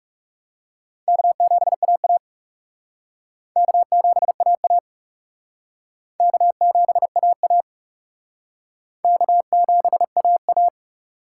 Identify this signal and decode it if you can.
Morse code, heard as a keyed tone, about 38 wpm, 700 Hz. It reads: K7AA K7AA K7AA K7AA
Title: Speed Racing - USA 1x2 Call Signs 30wpm